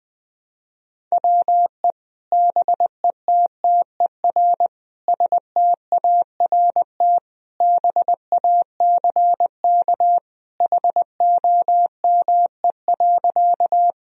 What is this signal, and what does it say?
Morse code, heard as a keyed tone, about 21 wpm, 700 Hz. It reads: WE BETTER START BACK HOME.
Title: Sentences from Top 1000 Words - Review Part 1 - 20wpm